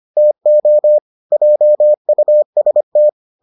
Morse code, heard as a keyed tone, about 25 wpm, 600 Hz. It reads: TO JUST